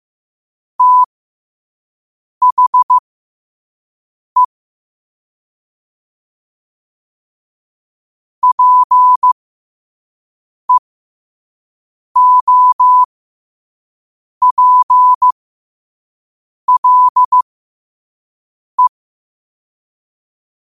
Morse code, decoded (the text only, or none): THE PEOPLE